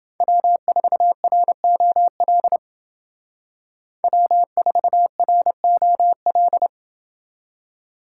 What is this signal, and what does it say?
Morse code, heard as a keyed tone, about 28 wpm, 700 Hz. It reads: W4ROL W4ROL